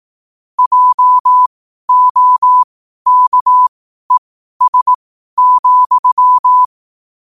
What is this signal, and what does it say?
Morse code, heard as a keyed tone, about 18 wpm, spaced as for 12 wpm, 1000 Hz. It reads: JOKES,
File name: qotdfiles / 12wpm